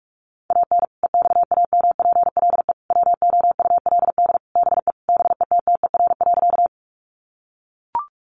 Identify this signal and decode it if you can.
Morse code, heard as a keyed tone, about 44 wpm, 700 Hz. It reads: AN EXAMPLE WOULD BE BETTER.
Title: Sentences from Top 400 Words 45wpm